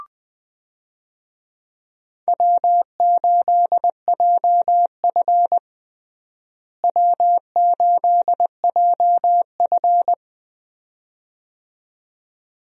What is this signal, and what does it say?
Morse code, heard as a keyed tone, about 21 wpm, 700 Hz. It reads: W8JF W8JF
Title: USA Call Signs - 4 Characters - Repeated 2x - 20wpm